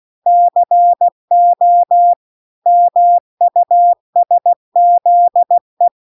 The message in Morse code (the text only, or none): CO MUSZE